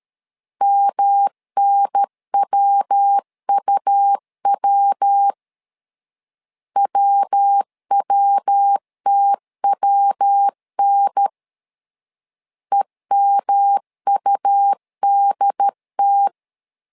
Morse code, heard as a keyed tone, about 13 wpm, 800 Hz. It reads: MNWUW WWTWN EMUDT
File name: LEZIONI / 3 MWRO / 3 MW 12wpm 1633